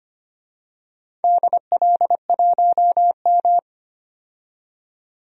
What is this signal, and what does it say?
Morse code, heard as a keyed tone, about 25 wpm, 700 Hz. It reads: DL1M
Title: DXCC-100 Call Signs - 4 Characters - 25wpm